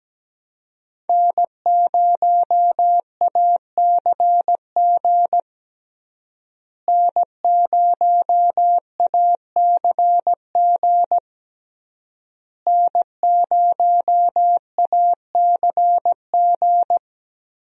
Morse code, heard as a keyed tone, about 17 wpm, 700 Hz. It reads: N0ACG N0ACG N0ACG
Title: USA Call Signs - 5 Characters - Repeated 3x - 17wpm